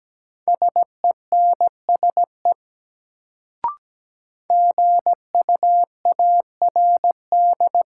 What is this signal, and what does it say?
Morse code, heard as a keyed tone, about 17 wpm, 700 Hz. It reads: SENSE GUARD
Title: Sentences from Top 700 Words - New Word Drill - 17wpm